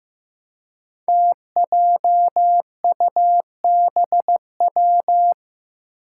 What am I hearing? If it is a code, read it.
Morse code, heard as a keyed tone, about 15 wpm, 700 Hz. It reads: TJUBW
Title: Five Challenging Letter-Number Combinations 15wpm